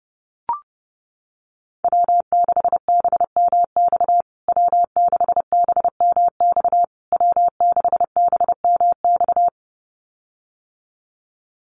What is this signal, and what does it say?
Morse code, heard as a keyed tone, about 30 wpm, 700 Hz. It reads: W6BMX W6BMX W6BMX
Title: USA Call Signs Repeated 3x 30wpm